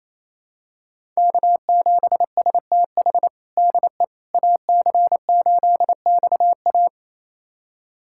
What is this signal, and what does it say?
Morse code, heard as a keyed tone, about 28 wpm, 700 Hz. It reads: K7STH DE AC8XA
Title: USA Callsign DE Callsign - 5 Characters - 28wpm